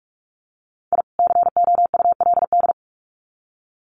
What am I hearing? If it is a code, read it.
Morse code, heard as a keyed tone, about 44 wpm, 700 Hz. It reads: I COULD
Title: Sets of 2 Words 45wpm